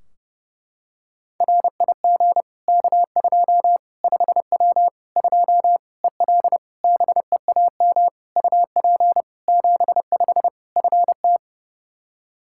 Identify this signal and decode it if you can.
Morse code, heard as a keyed tone, about 32 wpm, 700 Hz. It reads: RIG K2 5W 2 EL BEAM UP 75 FT
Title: Practice QSO - Exchange 2 - Info - 30wpm